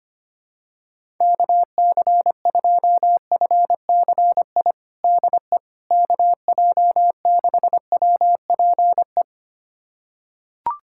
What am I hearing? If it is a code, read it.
Morse code, heard as a keyed tone, about 25 wpm, 700 Hz. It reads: KC2FCI DE KJ6WPE
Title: USA Callsign DE Callsign 25wpm